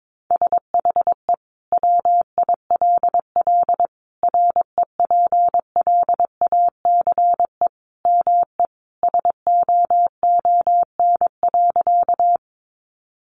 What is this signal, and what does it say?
Morse code, heard as a keyed tone, about 22 wpm, 700 Hz. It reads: SHE WILL REPLACE ME SOON.
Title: Sentences from Top 1400 Words 22wpm